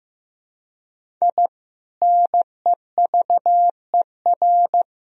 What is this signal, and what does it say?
Morse code, heard as a keyed tone, about 16 wpm, 700 Hz. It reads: I NEVER
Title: Sentences from Top 1000 Words - Review Part 1 - 15wpm